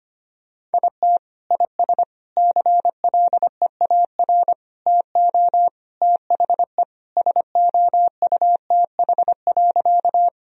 Morse code, heard as a keyed tone, about 25 wpm, 700 Hz. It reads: IT IS CLEAR TO THE SOUTH.
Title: Sentences from Top 1000 Words - Review Part 2 - 25wpm